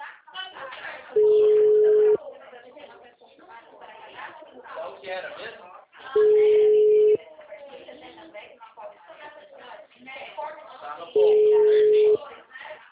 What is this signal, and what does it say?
Ringback tone (Europe, 425 Hz), with speech.
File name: out- -2030- - - .83087